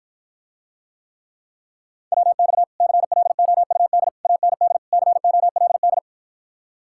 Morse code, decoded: WX CLOUD AND COLD

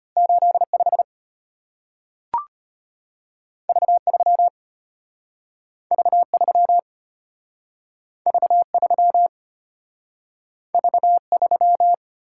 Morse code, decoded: 85 V3 V3 V3 V3

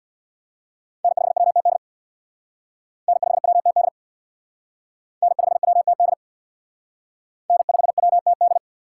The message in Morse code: N5WTD N5WTD N5WTD N5WTD

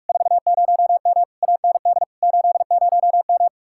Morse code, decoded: 40M AND 80M

